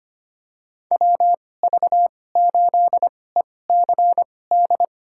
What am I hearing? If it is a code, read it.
Morse code, heard as a keyed tone, about 25 wpm, spaced as for 18 wpm, 700 Hz. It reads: WV8ECD
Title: USA Call Signs - 2x Character Spacing - 25wpm